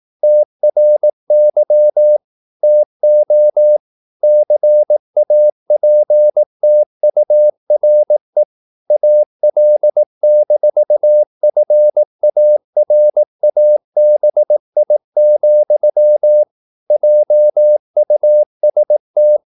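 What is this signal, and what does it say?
Morse code, heard as a keyed tone, about 18 wpm, 600 Hz. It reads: TRY TO CAPTURE AL-FARABI, JUST
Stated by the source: Morse practice at 18 wpm, English text, from CW med Gnister.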